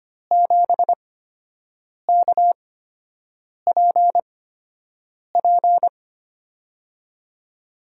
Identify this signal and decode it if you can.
Morse code, heard as a keyed tone, about 25 wpm, spaced as for 7 wpm, 700 Hz. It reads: 7KPP
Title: USA Call Signs - 8x Character Spacing - 25wpm